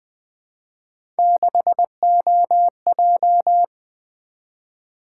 Morse code, heard as a keyed tone, about 20 wpm, 700 Hz. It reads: 6OJ
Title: Three Letter-Number Combinations - 20wpm